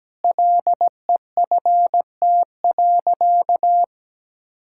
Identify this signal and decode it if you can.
Morse code, heard as a keyed tone, about 17 wpm, 700 Hz. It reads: LEFT.
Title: Sentences from Top 1000 Words - Review Part 6 - 17wpm